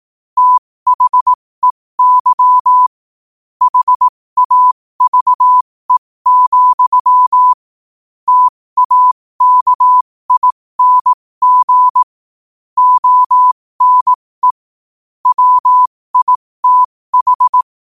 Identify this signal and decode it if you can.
Morse code, heard as a keyed tone, about 18 wpm, spaced as for 15 wpm, 1000 Hz. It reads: THEY HAVE, TAKING ONE WITH
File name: qotdfiles / 15wpm